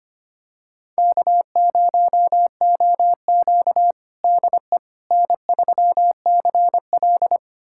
Morse code, heard as a keyed tone, about 25 wpm, 700 Hz. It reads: K0OQ DE N3CL
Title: USA Callsign DE Callsign - 4 Characters - 25wpm